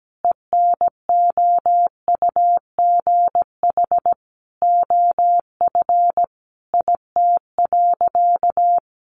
Morse code, heard as a keyed tone, about 17 wpm, 700 Hz. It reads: ENOUGH OF IT.
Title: Sentences from Top 400 Words 17wpm